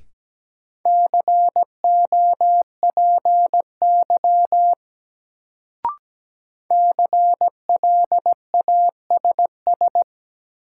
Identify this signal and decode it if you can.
Morse code, heard as a keyed tone, about 17 wpm, 700 Hz. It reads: COPY CLASS
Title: Sentences from Top 1400 Words - New Word Drill Alt - 17wpm